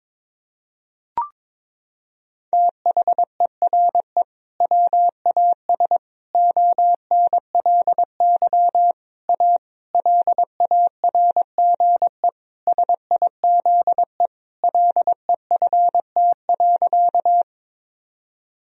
Morse code, heard as a keyed tone, about 21 wpm, 700 Hz. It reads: THERE WAS ONLY A LARGE SIZE LEFT.
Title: Sentences from Top 1000 Words - Review Part 2 - 22wpm